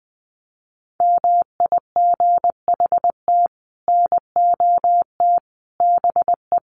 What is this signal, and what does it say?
Morse code, heard as a keyed tone, about 20 wpm, 700 Hz. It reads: MIGHT NOT BE
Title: Sets of 5 Words Encore2 20wpm